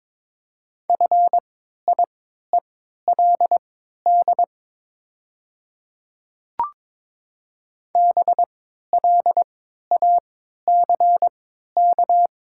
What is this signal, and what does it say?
Morse code, heard as a keyed tone, about 22 wpm, spaced as for 12 wpm, 700 Hz. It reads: FIELD BLACK
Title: 3x Character Spacing - Top 500 Words - 22wpm